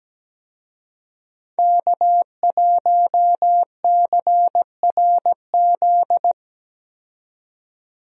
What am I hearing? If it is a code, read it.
Morse code, heard as a keyed tone, about 17 wpm, 700 Hz. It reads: K1CRZ
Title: USA Call Signs - 5 Characters - 17wpm